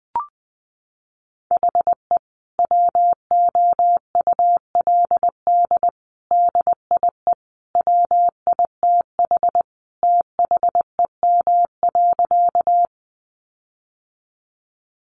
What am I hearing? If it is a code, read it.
Morse code, heard as a keyed tone, about 20 wpm, 700 Hz. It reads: HE WOULD DIE WITH THEM.
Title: Sentences from Top 1300 Words 20wpm